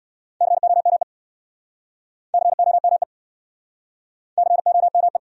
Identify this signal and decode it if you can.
Morse code, heard as a keyed tone, about 68 wpm, 700 Hz. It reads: COME COME COME